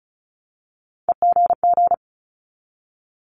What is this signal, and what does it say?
Morse code, heard as a keyed tone, about 36 wpm, 700 Hz. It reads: EGG
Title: Multi-letter Phonemes Repeated 3x 35wpm